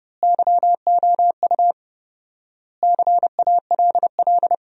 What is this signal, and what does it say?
Morse code, heard as a keyed tone, about 30 wpm, 700 Hz. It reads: YOU CALL
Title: 4x Word-Spacing - Sentences from Top 100 Words - 30wpm